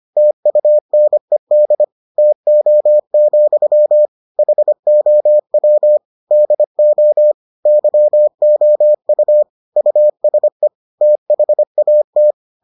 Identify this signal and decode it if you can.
Morse code, heard as a keyed tone, about 25 wpm, 600 Hz. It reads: TUNED TO, HOW DO YOU USE THAT